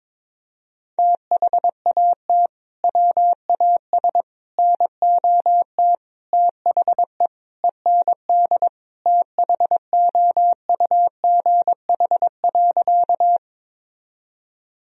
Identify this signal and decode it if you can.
Morse code, heard as a keyed tone, about 22 wpm, 700 Hz. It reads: THAT WAS NOT THE END THOUGH.
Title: Sentences from Top 1000 Words - Review Part 2 - 22wpm